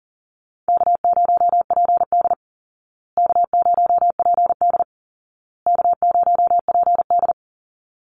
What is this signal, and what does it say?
Morse code, heard as a keyed tone, about 40 wpm, 700 Hz. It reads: K0PD K0PD K0PD
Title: Easy Call Signs Repeated 3x - 40wpm